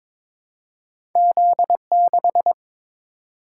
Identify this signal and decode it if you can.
Morse code, heard as a keyed tone, about 22 wpm, 700 Hz. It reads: Z6
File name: practice / 22wpm / Two Letter-Number Combinations - 22wpm